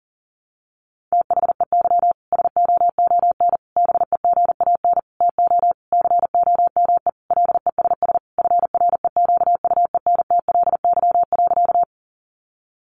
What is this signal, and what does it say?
Morse code, heard as a keyed tone, about 37 wpm, 700 Hz. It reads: THEY SOON BEGAN TO COME LESS FREQUENTLY.
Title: Sentences from Top 1300 Words 40wpm